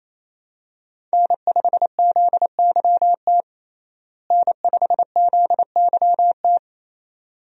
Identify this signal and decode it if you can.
Morse code, heard as a keyed tone, about 27 wpm, 700 Hz. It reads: N5ZYT N5ZYT
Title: USA Call Signs - 5 Characters - Repeated 2x - 28wpm